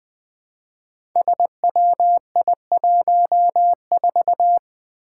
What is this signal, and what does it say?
Morse code, heard as a keyed tone, about 20 wpm, 700 Hz. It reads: SWI14
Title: Five Challenging Letter-Number Combinations 20wpm